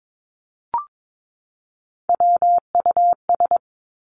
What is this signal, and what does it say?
Morse code, heard as a keyed tone, about 22 wpm, 700 Hz. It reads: WUS
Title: Three Challenging Letter-Number Combinations 22wpm